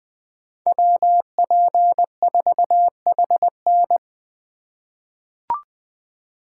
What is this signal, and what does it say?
Morse code, heard as a keyed tone, about 20 wpm, 700 Hz. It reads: WP4HN
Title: USA Call Signs - 5 Characters - 4x Character Spacing - 20wpm